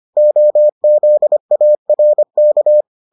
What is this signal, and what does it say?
Morse code, heard as a keyed tone, about 25 wpm, 600 Hz. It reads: OZARK